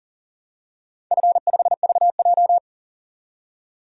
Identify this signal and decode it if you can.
Morse code, heard as a keyed tone, about 37 wpm, 700 Hz. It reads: F5VJ
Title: DXCC-100 Call Signs - 4 Characters - 40wpm